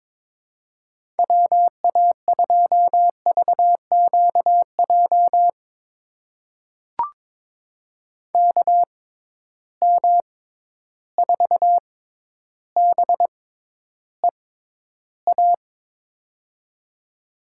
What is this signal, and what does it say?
Morse code, heard as a keyed tone, about 22 wpm, spaced as for 11 wpm, 700 Hz. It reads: WA2VQJ KM4BEA
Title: USA Call Signs - 6 Characters - 6x Character Spacing - 22wpm